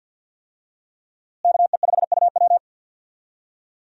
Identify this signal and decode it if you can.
Morse code, heard as a keyed tone, about 52 wpm, 700 Hz. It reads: KE5UW
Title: USA Call Signs - 5 Characters - Repeated 2x - 50wpm